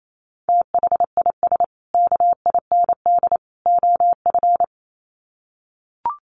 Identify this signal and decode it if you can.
Morse code, heard as a keyed tone, about 28 wpm, 700 Hz. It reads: THIS KIND OF
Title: Sets of 3 Words 28wpm